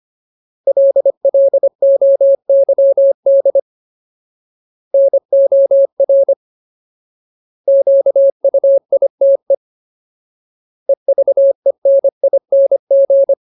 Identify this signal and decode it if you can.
Morse code, heard as a keyed tone, about 25 wpm, 550 Hz. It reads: LLOYD NOR QUITE EVENING
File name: Ultimate Word List / 25WPM Ultimate